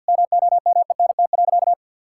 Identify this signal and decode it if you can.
Morse code, heard as a keyed tone, about 50 wpm, 700 Hz. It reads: MOMENT.
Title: Sentences from Top 1000 Words - Review Part 5 - 50wpm